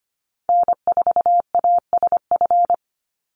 Morse code, heard as a keyed tone, about 25 wpm, 700 Hz. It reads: N4ASF
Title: Call Signs - 25wpm